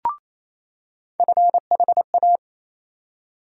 Morse code, heard as a keyed tone, about 28 wpm, 700 Hz. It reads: FHA
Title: USA Call Sign Postfixes - 3 Letters 28wpm